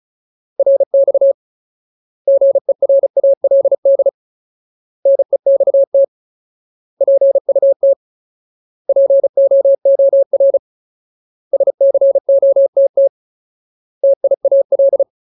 Morse code, heard as a keyed tone, about 34 wpm, 550 Hz. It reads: RX GERALD NEXT PUT POOR SCOTT TIAL
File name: Ultimate Word List / 35WPM Ultimate